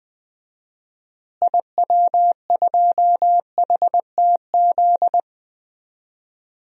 Morse code, heard as a keyed tone, about 20 wpm, 700 Hz. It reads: IW2HTZ
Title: DXCC-100 Call Signs - 6 Characters - Repeated 3x - 20wpm